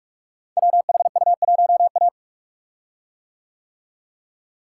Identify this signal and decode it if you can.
Morse code, heard as a keyed tone, about 44 wpm, 690 Hz. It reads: WHU1A
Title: Code Groups - 5 Letter:Numbers - 45wpm